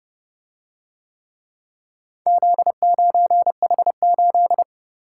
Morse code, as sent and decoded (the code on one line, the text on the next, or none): --.. ----. .... ---..
Z9H8